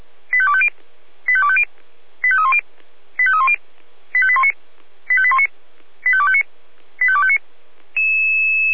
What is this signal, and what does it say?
Five-tone selective call (ZVEI), heard as quick runs of five tones, tones of about 70 ms. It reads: 85250 85250 85310 85310 85819 85819 85259 85259